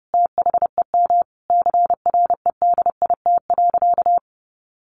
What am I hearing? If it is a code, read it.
Morse code, heard as a keyed tone, about 30 wpm, 700 Hz. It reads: THEM CREDIT.
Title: Sentences from Top 1000 Words 30wpm